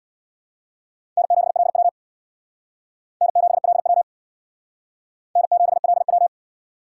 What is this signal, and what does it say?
Morse code, heard as a keyed tone, about 61 wpm, 700 Hz. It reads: N7LW N7LW N7LW
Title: Speed Racing - USA 1x2 Call Signs 50wpm